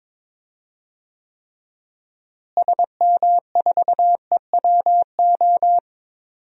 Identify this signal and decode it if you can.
Morse code, heard as a keyed tone, about 21 wpm, 700 Hz. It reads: SM4EWO
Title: DXCC-100 Call Signs - 6 Characters - Repeated 3x - 22wpm